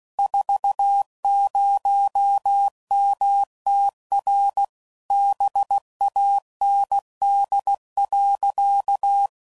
Morse code, heard as a keyed tone, about 16 wpm, 800 Hz. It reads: 40MTR BAND.
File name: CW Code Groups @15WPM / QSOs / WA7QOX QRX